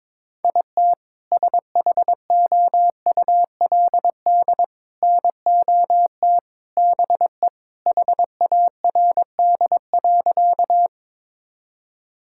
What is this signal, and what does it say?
Morse code, heard as a keyed tone, about 21 wpm, 700 Hz. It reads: IT SHOULD NOT BE HARD.
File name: practice / 22wpm / Sentences with 2-Word Phrases - 22wpm